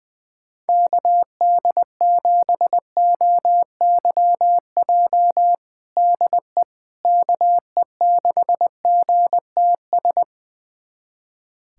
Morse code, heard as a keyed tone, about 21 wpm, 700 Hz. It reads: KD7OYJ DE KE6GTS
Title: USA Callsign DE Callsign - 6 Characters - 20wpm